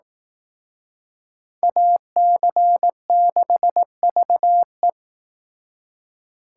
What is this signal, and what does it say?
Morse code, heard as a keyed tone, about 18 wpm, 700 Hz. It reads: AC6VE
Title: Speed Racing - USA 2x2 Call Signs 15wpm